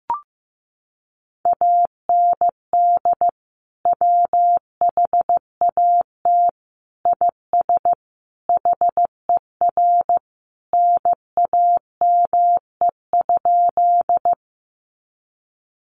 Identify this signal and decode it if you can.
Morse code, heard as a keyed tone, about 14 wpm, 700 Hz. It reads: AND WHAT IS HER NAME?
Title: Sentences from Top 200 Words 15wpm